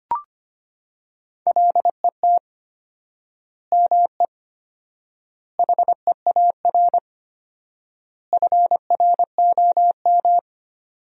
Morse code, heard as a keyed tone, about 25 wpm, 700 Hz. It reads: LET ME HEAR FROM
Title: 4x Word-Spacing - Sentences from Top 400 Words - 25wpm